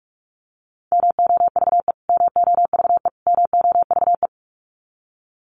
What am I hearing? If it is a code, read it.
Morse code, heard as a keyed tone, about 44 wpm, 700 Hz. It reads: MOVE MOVE MOVE
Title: Top 500 Words Repeated 45wpm